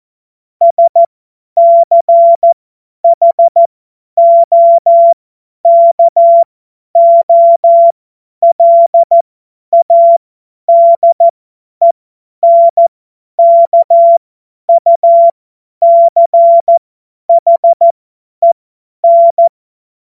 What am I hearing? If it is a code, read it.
Morse code, heard as a keyed tone, about 14 wpm, spaced as for 10 wpm, 680 Hz. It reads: SCHOKOLADENKUCHEN
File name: Morsekurs / Witze / Witz 13. April 2021, 50 F70